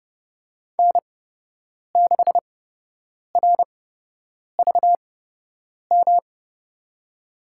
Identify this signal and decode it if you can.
Morse code, heard as a keyed tone, about 30 wpm, spaced as for 8 wpm, 700 Hz. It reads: N6RVM
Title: USA Call Signs - 5 Characters - 8x Character Spacing - 30wpm